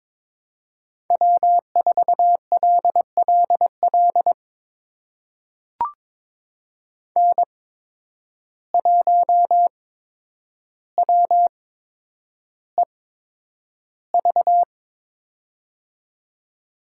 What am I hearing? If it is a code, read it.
Morse code, heard as a keyed tone, about 22 wpm, spaced as for 10 wpm, 700 Hz. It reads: W4LLL N1WEV
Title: USA Call Signs - 5 Characters - 8x Character Spacing - 22wpm